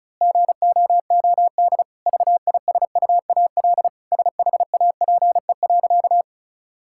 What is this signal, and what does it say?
Morse code, heard as a keyed tone, about 36 wpm, 700 Hz. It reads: GOOD VISUAL SHAPE.
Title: Sentences from Top 1000 Words - Review Part 6 - 35wpm